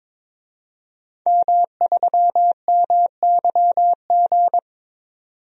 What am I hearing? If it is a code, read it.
Morse code, heard as a keyed tone, about 22 wpm, 700 Hz. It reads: M3MYG
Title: DXCC-100 Call Signs - 5 Characters - 22wpm